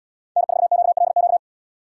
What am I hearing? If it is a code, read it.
Morse code, heard as a keyed tone, about 77 wpm, 700 Hz. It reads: N4QLQ